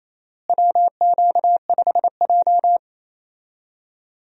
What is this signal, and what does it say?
Morse code, heard as a keyed tone, about 28 wpm, 700 Hz. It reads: WQ5J
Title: USA Call Signs - 4 Characters - Repeated 2x - 28wpm